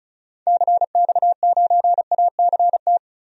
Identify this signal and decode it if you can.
Morse code, heard as a keyed tone, about 34 wpm, 700 Hz. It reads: CX9ACT
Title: DXCC-100 Call Signs - 6 Characters - Repeated 2x - 35wpm